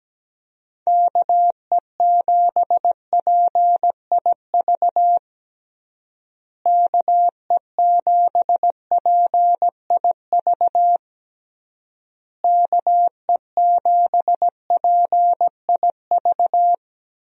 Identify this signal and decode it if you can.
Morse code, heard as a keyed tone, about 17 wpm, 700 Hz. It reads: KE7PIV KE7PIV KE7PIV
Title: USA Call Signs - 6 Characters - Repeated 3x - 17wpm